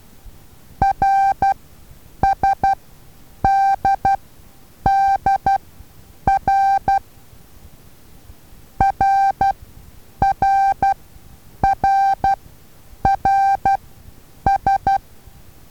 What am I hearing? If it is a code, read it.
Morse code, heard as a keyed tone, about 12 wpm, spaced as for 8 wpm, 780 Hz. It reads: RSDDR RRRRS
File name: Taviro HA6VA A